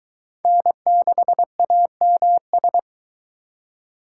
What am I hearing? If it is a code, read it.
Morse code, heard as a keyed tone, about 24 wpm, 700 Hz. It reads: N6AMS